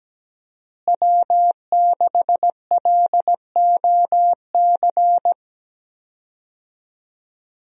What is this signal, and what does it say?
Morse code, heard as a keyed tone, about 17 wpm, 700 Hz. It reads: W6LOC
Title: USA Call Signs - 5 Characters - Repeated 2x - 17wpm